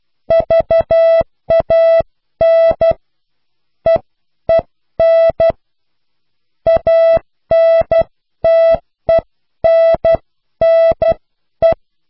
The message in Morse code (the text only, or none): VAN EEN ANTENNE